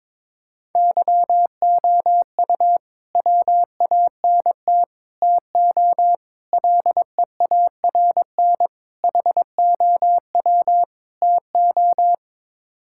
Morse code, heard as a keyed tone, about 22 wpm, 700 Hz. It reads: YOU WANT TO LEARN HOW TO
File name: practice / 22wpm / Sets of 6 Words - 22wpm